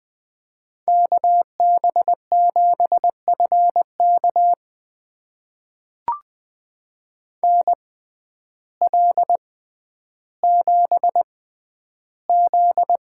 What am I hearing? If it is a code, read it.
Morse code, heard as a keyed tone, about 20 wpm, spaced as for 11 wpm, 700 Hz. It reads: KB7FK NL7Z
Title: USA Call Signs - 6x Character Spacing - 20wpm